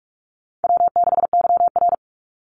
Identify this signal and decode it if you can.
Morse code, heard as a keyed tone, about 44 wpm, 700 Hz. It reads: W6YR
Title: Easy Call Signs - 45wpm